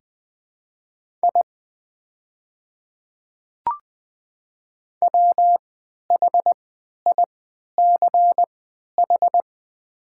Morse code, heard as a keyed tone, about 20 wpm, spaced as for 11 wpm, 690 Hz. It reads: I WHICH
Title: 3x Character Spacing - Top 100 Words - 20wpm